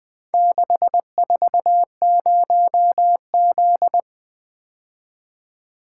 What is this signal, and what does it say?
Morse code, heard as a keyed tone, about 23 wpm, 700 Hz. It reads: 640Z
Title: Speed-Racing - Four Letter-Number Combinations - 20wpm